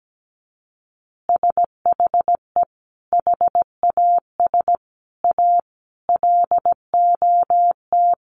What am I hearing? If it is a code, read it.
Morse code, heard as a keyed tone, about 17 wpm, 700 Hz. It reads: SHE HAS A LOT
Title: Additional Easy Phrases 17wpm